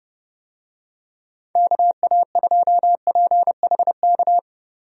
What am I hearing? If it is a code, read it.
Morse code, heard as a keyed tone, about 32 wpm, 700 Hz. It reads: KA2PHK
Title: USA Call Signs - 6 Characters - 30wpm